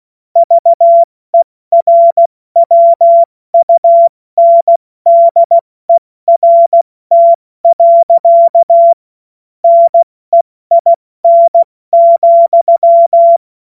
Morse code, heard as a keyed tone, about 16 wpm, 680 Hz. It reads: VERWUNDERT. NEIN,
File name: Morsekurs / Witze / Witz 27. Juni 2023 70 F80